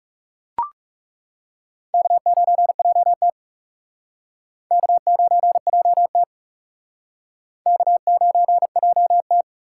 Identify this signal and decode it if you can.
Morse code, heard as a keyed tone, about 36 wpm, 700 Hz. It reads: K9JT K9JT K9JT
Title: Speed Racing - USA Call Signs 30wpm